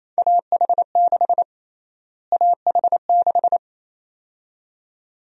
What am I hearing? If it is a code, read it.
Morse code, heard as a keyed tone, about 28 wpm, 700 Hz. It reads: AH6 AH6
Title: USA Call Sign Prefixes - Repeated 2x - 28wpm